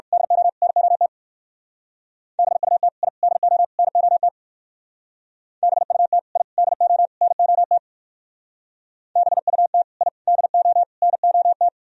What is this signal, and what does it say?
Morse code, heard as a keyed tone, about 54 wpm, 700 Hz. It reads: DO NOT BUT I DO NOT BUT I DO NOT BUT I DO NOT
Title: Speed-Racing - Sets of 4 Words 45wpm